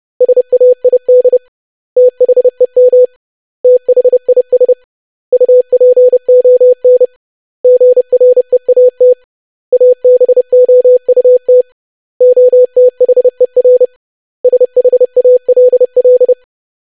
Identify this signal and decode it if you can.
Morse code, heard as a keyed tone, about 30 wpm, 500 Hz. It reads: SAID THEM THIS UPON GREAT ABOUT OTHER SHALL
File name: lesson / web / basic100-25wpm